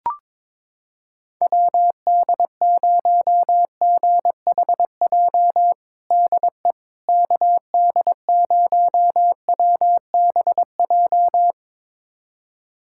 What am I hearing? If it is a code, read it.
Morse code, heard as a keyed tone, about 22 wpm, 700 Hz. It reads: WD0GHJ DE KD0WBJ
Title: USA Callsign DE Callsign - 6 Characters - 22wpm